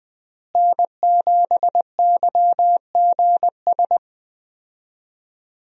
Morse code, heard as a keyed tone, about 21 wpm, 700 Hz. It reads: N7YGS
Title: USA Call Signs - 5 Characters - Repeated 2x - 20wpm